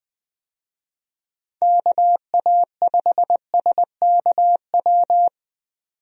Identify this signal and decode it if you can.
Morse code, heard as a keyed tone, about 20 wpm, 700 Hz. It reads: KA5SKW
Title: USA Call Signs - 6 Characters - 20wpm